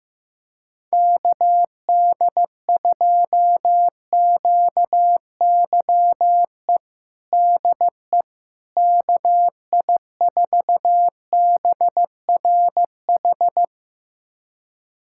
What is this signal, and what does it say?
Morse code, heard as a keyed tone, about 15 wpm, 700 Hz. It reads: KD2QYE DE KI4BRH
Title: USA Callsign DE Callsign 15wpm